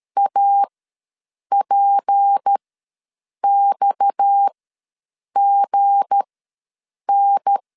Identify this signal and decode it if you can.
Morse code, heard as a keyed tone, about 13 wpm, spaced as for 7 wpm, 800 Hz. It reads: APXGN
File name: LEZIONI / 7 GF9Z / 7 G ws 12wpm 3140